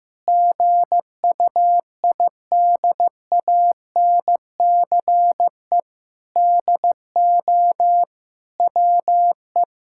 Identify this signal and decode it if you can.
Morse code, heard as a keyed tone, about 15 wpm, 700 Hz. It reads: GUIDANCE DO WE
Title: Sentences from Top 1600 Words 15wpm